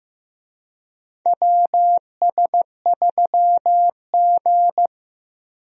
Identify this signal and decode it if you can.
Morse code, heard as a keyed tone, about 16 wpm, 700 Hz. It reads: WS3G
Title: USA Call Signs - 4 Characters - Repeated 2x - 15wpm